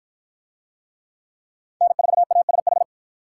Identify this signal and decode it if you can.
Morse code, heard as a keyed tone, about 53 wpm, 700 Hz. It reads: N4ASH